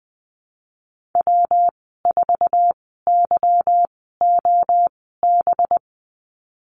Morse code, heard as a keyed tone, about 19 wpm, spaced as for 14 wpm, 700 Hz. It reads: W4YOB
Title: 186 Lesson 30g - Letter b - Call Signs 20wpm